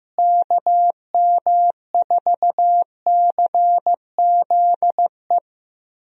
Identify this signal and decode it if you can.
Morse code, heard as a keyed tone, about 14 wpm, 700 Hz. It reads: KM4CZE
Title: USA Call Signs - 6 Characters - Repeated 3x - 15wpm